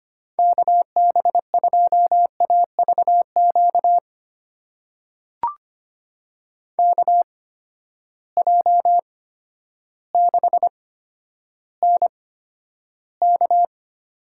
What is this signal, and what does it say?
Morse code, heard as a keyed tone, about 25 wpm, spaced as for 11 wpm, 700 Hz. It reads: KB2AVQ KJ6NK
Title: USA Call Signs - 6 Characters - 8x Character Spacing - 25wpm